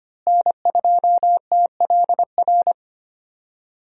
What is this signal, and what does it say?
Morse code, heard as a keyed tone, about 25 wpm, 700 Hz. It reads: N2TLR
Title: USA Call Signs - 5 Characters - Repeated 2x - 25wpm